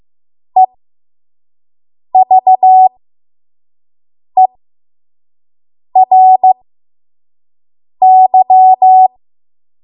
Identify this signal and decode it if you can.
Morse code, heard as a keyed tone, about 15 wpm, spaced as for 5 wpm, 750 Hz. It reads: EVERY
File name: Morse / 05WPM